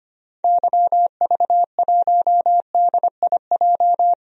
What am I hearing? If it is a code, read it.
Morse code, heard as a keyed tone, about 25 wpm, 700 Hz. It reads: YV1DIJ
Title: DXCC-100 Call Signs - 6 Characters - 25wpm